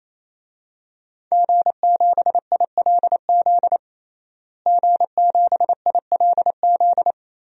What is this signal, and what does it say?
Morse code, heard as a keyed tone, about 28 wpm, 700 Hz. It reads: G7ILZ G7ILZ
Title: DXCC-100 Call Signs - 5 Characters - Repeated 3x - 28wpm